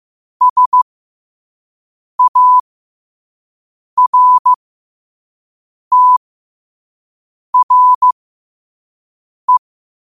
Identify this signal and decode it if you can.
Morse code, heard as a keyed tone, about 15 wpm, spaced as for 5 wpm, 1000 Hz. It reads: SARTRE
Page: New quotes every day in morse code at 5 Words per minute.